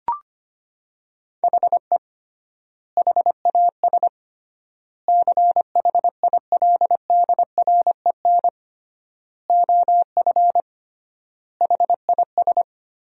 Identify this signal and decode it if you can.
Morse code, heard as a keyed tone, about 25 wpm, 700 Hz. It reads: HE HAS CHILDREN OF HIS
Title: 3x Word-Spacing - Sentences from Top 300 Words - 25wpm